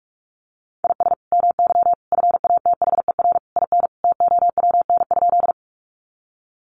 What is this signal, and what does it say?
Morse code, heard as a keyed tone, about 44 wpm, 700 Hz. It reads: IS MY FATHER IN TOWN?
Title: Sentences from Top 500 Words 45wpm